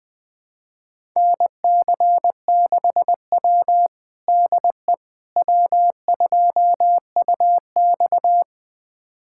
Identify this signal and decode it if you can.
Morse code, heard as a keyed tone, about 21 wpm, 700 Hz. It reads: NC6W DE W2UX
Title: USA Callsign DE Callsign - 4 Characters - 20wpm